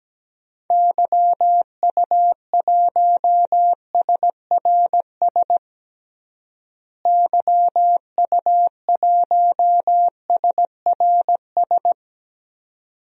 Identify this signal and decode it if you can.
Morse code, heard as a keyed tone, about 17 wpm, 700 Hz. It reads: YU1SRS YU1SRS
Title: DXCC-100 Call Signs - 6 Characters - Repeated 2x - 17wpm